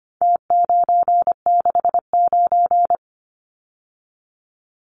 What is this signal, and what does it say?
Morse code, heard as a keyed tone, about 25 wpm, 700 Hz. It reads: T969
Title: Numbers - Contest Style - 4-Digit 25wpm